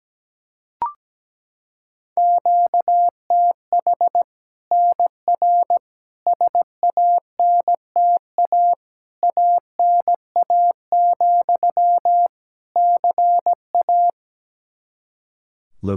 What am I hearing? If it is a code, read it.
Morse code, heard as a keyed tone, about 17 wpm, 700 Hz. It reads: QTH NR SANTA ANA, CA
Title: Practice QSO - Exchange 1 - Beginning - 17wpm